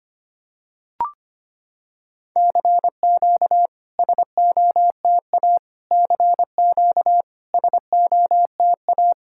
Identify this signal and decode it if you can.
Morse code, heard as a keyed tone, about 25 wpm, 690 Hz. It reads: CQ SOTA CQ SOTA
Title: SOTA QSO Practice 25wpm